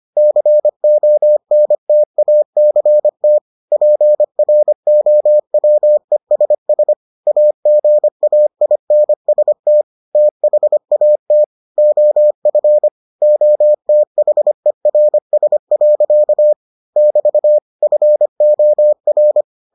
Morse code, heard as a keyed tone, about 25 wpm, 600 Hz. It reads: CONTACT PROWESS AGAINST THAT OF OTHERS. <BT> FOR